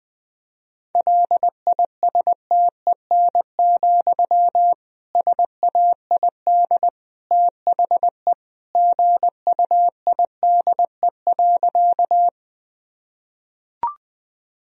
Morse code, heard as a keyed tone, about 20 wpm, 700 Hz. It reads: LISTEN, SAID THE GUIDE.
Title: Sentences from Top 1000 Words - Review Part 3 - 20wpm